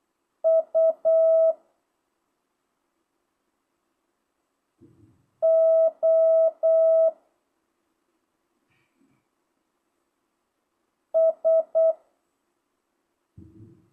Morse code, heard as a keyed tone, about 8 wpm, 650 Hz. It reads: U O S
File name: jh8cbh / 2025cw